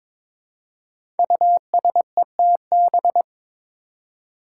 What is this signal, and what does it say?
Morse code, heard as a keyed tone, about 22 wpm, 700 Hz. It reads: USETB
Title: Five Challenging Letter-Number Combinations 22wpm